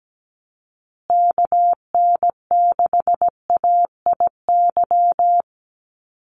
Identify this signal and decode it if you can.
Morse code, heard as a keyed tone, about 17 wpm, 700 Hz. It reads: KN6AIY
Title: USA Call Signs Repeated 2x 17wpm